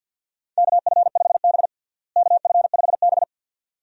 Morse code, heard as a keyed tone, about 52 wpm, 700 Hz. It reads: KF5B KF5B
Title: USA Call Signs - 4 Characters - Repeated 2x - 50wpm